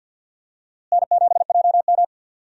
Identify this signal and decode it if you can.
Morse code, heard as a keyed tone, about 49 wpm, 700 Hz. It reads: N7JM
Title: USA Call Signs - 4 Characters - Repeated 3x - 50wpm